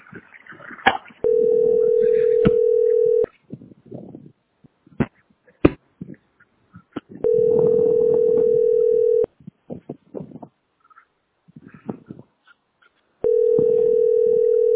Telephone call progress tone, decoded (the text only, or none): ringback tone